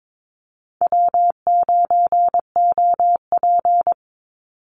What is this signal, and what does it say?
Morse code, heard as a keyed tone, about 22 wpm, 700 Hz. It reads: W9OP
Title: Call Signs - 22wpm